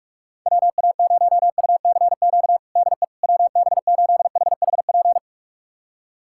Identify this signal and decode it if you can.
Morse code, heard as a keyed tone, about 44 wpm, 690 Hz. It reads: WA0UCQ DE WB8HHP
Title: USA Callsign DE Callsign - 6 Characters - 45wpm